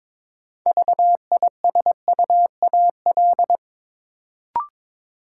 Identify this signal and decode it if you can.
Morse code, heard as a keyed tone, about 22 wpm, 700 Hz. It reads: VISUAL
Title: Sets of 5 Words Encore - New Word Drill - 22wpm